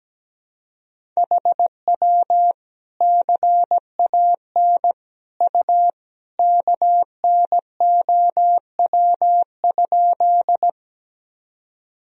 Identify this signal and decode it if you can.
Morse code, heard as a keyed tone, about 17 wpm, 700 Hz. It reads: HW CAN U KNOW?
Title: Sentences from Top 100 Words and Abbreviations 17wpm